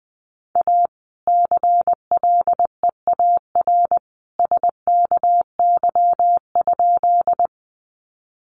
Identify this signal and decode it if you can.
Morse code, heard as a keyed tone, about 20 wpm, 700 Hz. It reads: A CLEAR SKY?
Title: Sentences from Top 1200 Words 20wpm